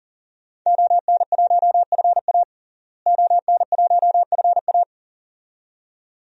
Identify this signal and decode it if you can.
Morse code, heard as a keyed tone, about 37 wpm, 700 Hz. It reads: ON1FA ON1FA
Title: DXCC-100 Call Signs - 5 Characters - Repeated 2x - 40wpm